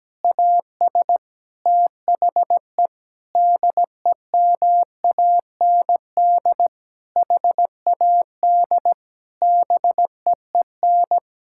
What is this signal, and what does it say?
Morse code, heard as a keyed tone, about 17 wpm, 700 Hz. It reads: AS THE DEMAND HAD BEEN
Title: Sentences from Top 1000 Words - Review Part 6 - 17wpm